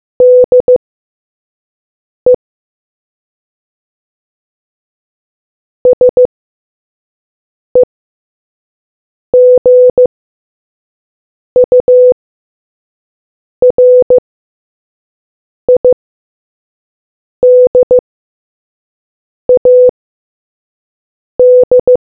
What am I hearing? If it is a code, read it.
Morse code, heard as a keyed tone, about 15 wpm, spaced as for 5 wpm, 500 Hz. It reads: DE SEGURIDAD